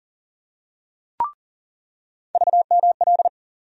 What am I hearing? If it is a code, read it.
Morse code, heard as a keyed tone, about 40 wpm, 700 Hz. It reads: VML
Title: USA Call Sign Postfixes - 3 Letters 40wpm